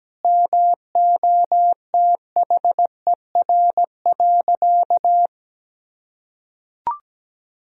Morse code, heard as a keyed tone, about 17 wpm, 700 Hz. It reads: MOTHER.
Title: Sentences from Top 1000 Words - Review Part 3 - 17wpm